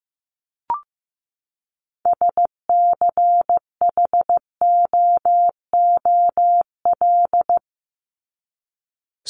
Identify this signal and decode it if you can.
Morse code, heard as a keyed tone, about 15 wpm, 700 Hz. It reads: SCHOOL
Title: Top 300 Words 15wpm